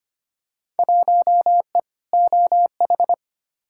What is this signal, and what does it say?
Morse code, heard as a keyed tone, about 25 wpm, 700 Hz. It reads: 1E OH
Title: Field Day Exchanges - 25wpm